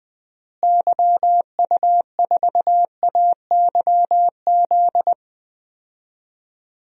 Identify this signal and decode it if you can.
Morse code, heard as a keyed tone, about 20 wpm, 700 Hz. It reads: YU4AYZ
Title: DXCC-100 Call Signs - 6 Characters - Repeated 3x - 20wpm